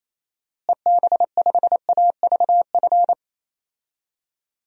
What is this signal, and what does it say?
Morse code, heard as a keyed tone, about 28 wpm, 700 Hz. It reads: EB5AVF
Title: DXCC-100 Call Signs - 6 Characters - Repeated 2x - 28wpm